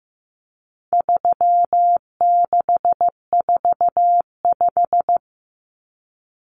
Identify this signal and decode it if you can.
Morse code, heard as a keyed tone, about 15 wpm, 700 Hz. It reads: 3645
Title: Common Cut Numbers - 4-Digit 15wpm